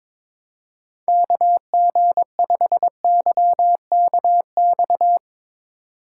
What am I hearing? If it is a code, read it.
Morse code, heard as a keyed tone, about 24 wpm, 700 Hz. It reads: KG5YKX
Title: Speed Racing - USA 2x3 Call Signs 22wpm